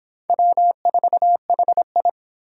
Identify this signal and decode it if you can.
Morse code, heard as a keyed tone, about 24 wpm, 700 Hz. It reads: W4HI